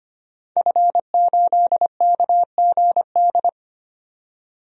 Morse code, heard as a keyed tone, about 25 wpm, 700 Hz. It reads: F8KGD
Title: DXCC-100 Call Signs - 5 Characters - 25wpm